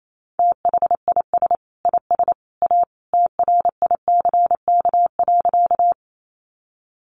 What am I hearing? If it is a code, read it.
Morse code, heard as a keyed tone, about 28 wpm, 700 Hz. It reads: THIS IS A TRICK.
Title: Sentences from Top 1500 Words 28wpm